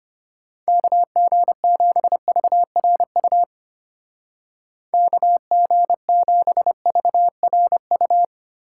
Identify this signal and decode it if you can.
Morse code, heard as a keyed tone, about 32 wpm, 700 Hz. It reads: KG7VRU KG7VRU